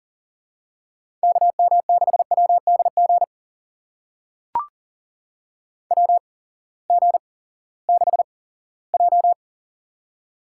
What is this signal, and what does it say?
Morse code, heard as a keyed tone, about 40 wpm, spaced as for 20 wpm, 700 Hz. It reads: KM6WDG WG6J
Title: USA Call Signs - 8x Character Spacing - 40wpm